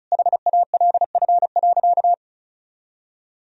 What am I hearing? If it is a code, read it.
Morse code, heard as a keyed tone, about 36 wpm, 700 Hz. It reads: HALF.
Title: Sentences from Top 1000 Words - Review Part 2 - 35wpm